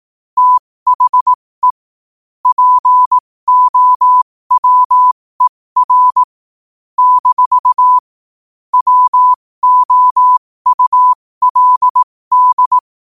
Morse code, heard as a keyed tone, about 18 wpm, spaced as for 15 wpm, 1000 Hz. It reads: THE POWER - WOULD
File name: qotdfiles / 15wpm